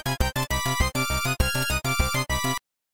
There is no code.